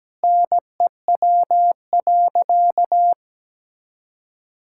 Morse code, heard as a keyed tone, about 17 wpm, 700 Hz. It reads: NEW.
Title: Sentences from Top 1600 Words 17wpm